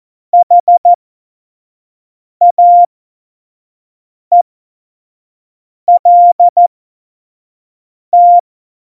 Morse code, heard as a keyed tone, about 14 wpm, spaced as for 5 wpm, 700 Hz. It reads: HAELT
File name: Morsekurs / Witze / Witz 26. Februar 2019, 25 F70